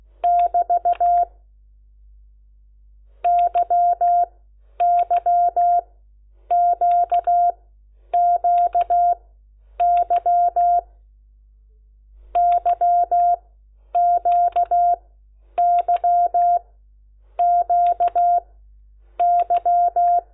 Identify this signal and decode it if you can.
Morse code, heard as a keyed tone, about 16 wpm, spaced as for 9 wpm, 700 Hz. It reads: <BT> YYQQY YQYQY
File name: Lecciones / CLASE Nº10